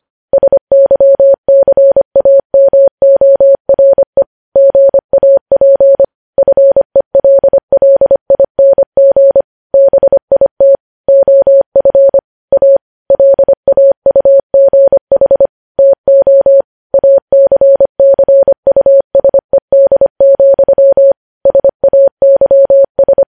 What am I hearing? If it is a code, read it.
Morse code, heard as a keyed tone, about 25 wpm, 550 Hz. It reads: SYCAMORE GAP FELLING BIT OF A LAUGH TO ACCUSED, SAYS
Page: news headlines in morse code at 25 wpm